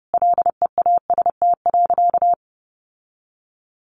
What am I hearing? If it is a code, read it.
Morse code, heard as a keyed tone, about 30 wpm, 700 Hz. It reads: LEAST.
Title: Sentences from Top 900 Words 30wpm